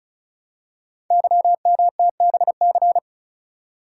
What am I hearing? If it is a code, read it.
Morse code, heard as a keyed tone, about 36 wpm, 690 Hz. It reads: YMTBC
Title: Code Groups - 5 Letters - 35wpm